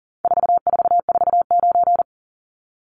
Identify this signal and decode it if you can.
Morse code, heard as a keyed tone, about 40 wpm, 700 Hz. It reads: 4449
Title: Numbers - 4-Digit 40wpm